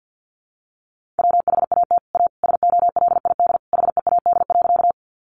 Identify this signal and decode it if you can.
Morse code, heard as a keyed tone, about 50 wpm, 700 Hz. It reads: WHAT A SOLID HEAD.
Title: Sentences from Top 1500 Words 50wpm